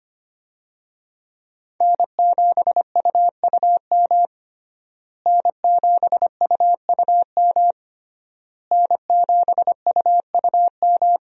Morse code, heard as a keyed tone, about 25 wpm, 700 Hz. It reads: N7UUM N7UUM N7UUM
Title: USA Call Signs - 5 Characters - Repeated 3x - 25wpm